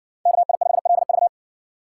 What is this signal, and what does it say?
Morse code, heard as a keyed tone, about 57 wpm, 700 Hz. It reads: KI5LV